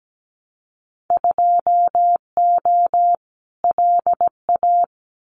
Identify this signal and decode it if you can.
Morse code, heard as a keyed tone, about 17 wpm, 700 Hz. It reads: 2O LA
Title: Winter Field Day Exchanges 17wpm